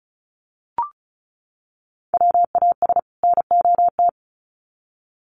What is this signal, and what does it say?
Morse code, heard as a keyed tone, about 34 wpm, 700 Hz. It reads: WAS NOT
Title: Sets of 2 Words 35wpm